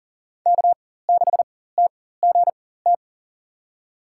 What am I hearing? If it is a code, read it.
Morse code, heard as a keyed tone, about 40 wpm, spaced as for 19 wpm, 700 Hz. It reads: K6TGT
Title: USA Call Signs - 5 Characters - 4x Character Spacing - 40wpm